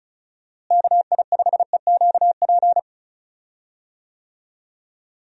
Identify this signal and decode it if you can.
Morse code, heard as a keyed tone, about 34 wpm, 700 Hz. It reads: KI5EQP
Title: USA Call Signs - 6 Characters - Repeated 3x - 35wpm